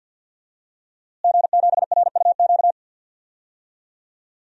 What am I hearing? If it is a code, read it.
Morse code, heard as a keyed tone, about 47 wpm, 700 Hz. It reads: G7RUQ
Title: DXCC-100 Call Signs - 5 Characters - Repeated 2x - 50wpm